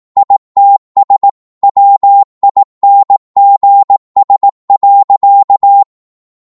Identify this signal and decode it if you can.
Morse code, heard as a keyed tone, about 18 wpm, 800 Hz. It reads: ITS WINGS.
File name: 18wpm